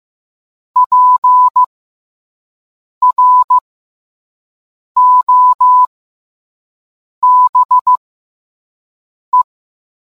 Morse code, heard as a keyed tone, about 15 wpm, spaced as for 5 wpm, 1000 Hz. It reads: PROBE